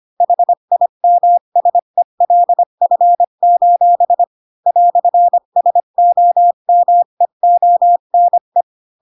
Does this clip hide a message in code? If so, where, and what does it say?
Yes: Morse code from 0.2 to 8.6 s, heard as a keyed tone, about 25 wpm, 700 Hz. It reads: HIMSELF: "SOMEONE